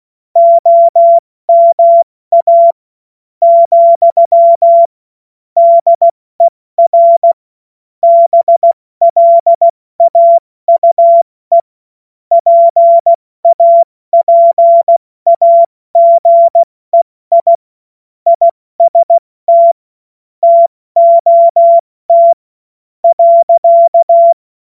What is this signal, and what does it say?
Morse code, heard as a keyed tone, about 16 wpm, spaced as for 13 wpm, 680 Hz. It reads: OMA, DER BLAUE PAPAGEI IST TOT.
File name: Morsekurs / Witze / Witz 28. September 2021, 70 F80